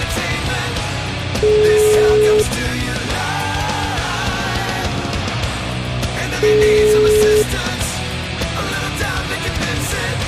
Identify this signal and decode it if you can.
Ringback tone (Europe, 425 Hz), with noise.